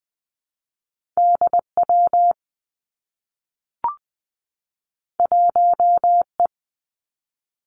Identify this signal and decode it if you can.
Morse code, heard as a keyed tone, about 20 wpm, 700 Hz. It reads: D W 1 E
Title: Two Challenging Letter-Number Combinations 20wpm